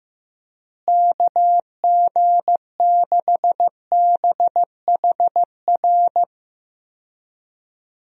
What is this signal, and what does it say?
Morse code, heard as a keyed tone, about 15 wpm, 700 Hz. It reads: KG6BHR
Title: USA Call Signs - 6 Characters - Repeated 3x - 15wpm